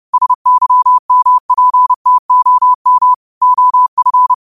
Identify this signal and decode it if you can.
Morse code, heard as a keyed tone, about 30 wpm, 1000 Hz. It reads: SYMPTOM OF